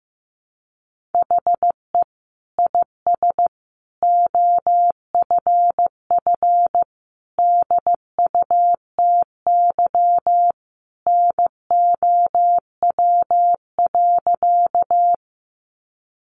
Morse code, heard as a keyed tone, about 15 wpm, 700 Hz. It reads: HE IS OFF DUTY NOW.
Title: Sentences from Top 1500 Words 15wpm